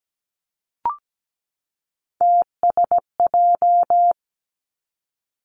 Morse code, heard as a keyed tone, about 17 wpm, 700 Hz. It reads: TSJ
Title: Three Challenging Letter-Number Combinations 17wpm